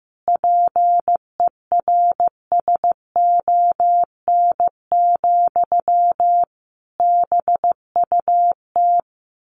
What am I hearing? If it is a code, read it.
Morse code, heard as a keyed tone, about 15 wpm, 700 Hz. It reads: PERSON, BUT
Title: Sentences from Top 1400 Words 15wpm